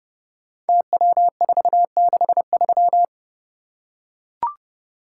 Morse code, heard as a keyed tone, about 30 wpm, 700 Hz. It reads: TW463
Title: Code Groups - 5 Letter:Numbers - 30wpm